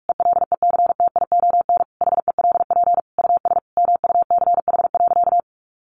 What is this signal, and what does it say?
Morse code, heard as a keyed tone, about 44 wpm, 700 Hz. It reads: ELECTION HELP US MUCH.
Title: Sentences from Top 1500 Words 45wpm